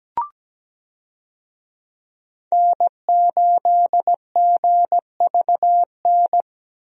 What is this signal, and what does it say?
Morse code, heard as a keyed tone, about 17 wpm, 700 Hz. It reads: N8GVN
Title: USA Call Signs - 5 Characters - 17wpm